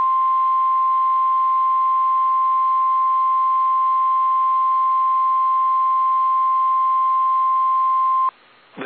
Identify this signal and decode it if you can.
Emergency Alert System (SAME) tones (the 1050 Hz attention tone). It is heard from 1.0 to 8.1 s.
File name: eas / KSDZKDJL / KSDZKDJL